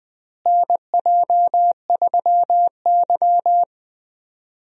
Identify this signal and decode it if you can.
Morse code, heard as a keyed tone, about 20 wpm, 700 Hz. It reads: NJ3Y
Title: USA Call Signs - 4 Characters - Repeated 3x - 20wpm